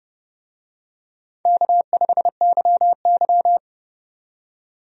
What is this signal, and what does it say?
Morse code, heard as a keyed tone, about 30 wpm, 700 Hz. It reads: K5YY
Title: USA Call Signs - 4 Characters - 30wpm